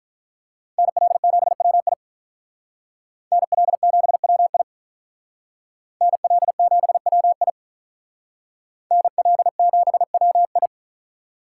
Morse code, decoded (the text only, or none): NL7WI NL7WI NL7WI NL7WI